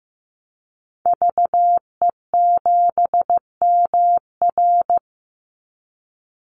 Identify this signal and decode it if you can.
Morse code, heard as a keyed tone, about 15 wpm, 700 Hz. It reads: VE7MR
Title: Call Signs Repeated 3x - 15wpm